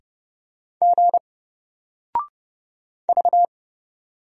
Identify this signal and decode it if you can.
Morse code, heard as a keyed tone, about 29 wpm, 700 Hz. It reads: G V
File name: Single-Letter-Number-30wpm